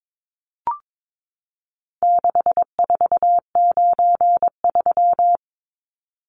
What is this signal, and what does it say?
Morse code, heard as a keyed tone, about 22 wpm, 700 Hz. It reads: 6493
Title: Numbers - 4-Digit 22wpm